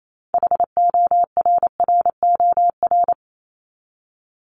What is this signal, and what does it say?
Morse code, heard as a keyed tone, about 28 wpm, 700 Hz. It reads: HORROR
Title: Top 1501-1750 Words 28wpm